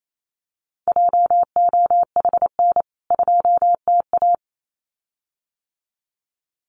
Morse code, heard as a keyed tone, about 28 wpm, 700 Hz. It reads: JOHN 2TA
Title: CWT Exchange 28wpm